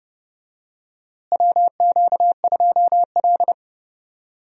Morse code, heard as a keyed tone, about 31 wpm, 700 Hz. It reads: WQ2L
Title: USA Call Signs - 4 Characters - Repeated 2x - 30wpm